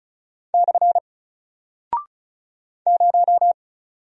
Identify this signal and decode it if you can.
Morse code, heard as a keyed tone, about 36 wpm, 700 Hz. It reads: / 0
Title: Single Character (Letter:Number:Punctuation:Prosign) Alt 35wpm